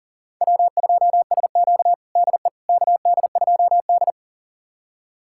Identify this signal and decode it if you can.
Morse code, heard as a keyed tone, about 40 wpm, 700 Hz. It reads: W2SQ DE KD2D
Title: USA Callsign DE Callsign - 4 Characters - 40wpm